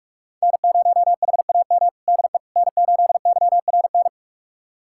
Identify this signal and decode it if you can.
Morse code, heard as a keyed tone, about 46 wpm, 700 Hz. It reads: N0HAM DE N8YRN
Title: USA Callsign DE Callsign - 5 Characters - 45wpm